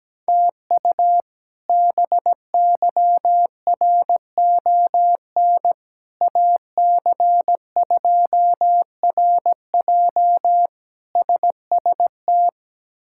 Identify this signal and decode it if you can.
Morse code, heard as a keyed tone, about 17 wpm, 700 Hz. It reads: TU BYRON AC2RJ SST
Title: K1USN SST QSO Practice - 17wpm